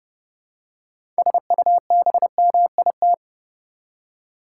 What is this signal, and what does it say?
Morse code, heard as a keyed tone, about 31 wpm, 700 Hz. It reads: SUBMIT
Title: Six Letter Words - 30wpm